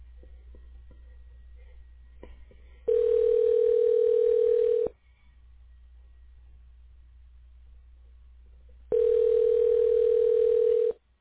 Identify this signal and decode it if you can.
Ringback tone (North America), 440 + 480 Hz.